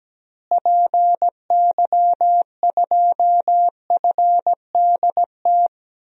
Morse code, heard as a keyed tone, about 17 wpm, 700 Hz. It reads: PY2FDT
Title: DXCC-100 Call Signs - 6 Characters - Repeated 2x - 17wpm